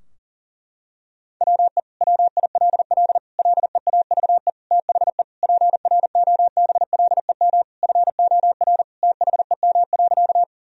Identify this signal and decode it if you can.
Morse code, heard as a keyed tone, about 40 wpm, 700 Hz. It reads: WE WILL LEAVE THE PROBLEM FOR THEM.
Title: Sentences from Top 1000 Words - Review Part 2 - 40wpm